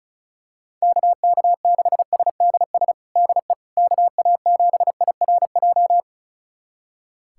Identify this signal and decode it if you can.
Morse code, heard as a keyed tone, about 36 wpm, 700 Hz. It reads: KK6SDS DE KA7IRJ
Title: USA Callsign DE Callsign - 6 Characters - 35wpm